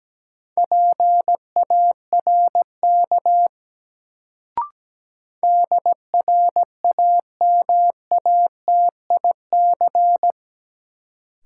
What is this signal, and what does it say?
Morse code, heard as a keyed tone, about 17 wpm, 700 Hz. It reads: PARK DRAMATIC
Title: Sentences from Top 1300 Words - New Word Drill Alt - 17wpm